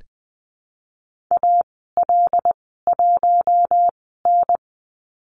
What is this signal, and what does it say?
Morse code, heard as a keyed tone, about 20 wpm, spaced as for 14 wpm, 700 Hz. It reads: AL1N
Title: 082 Lesson 14g - Number 2 - Call Signs 20wpm